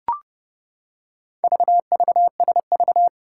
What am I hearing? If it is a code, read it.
Morse code, heard as a keyed tone, about 30 wpm, 700 Hz. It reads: VVSV
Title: Four Challenging Letter-Number Combinations 30wpm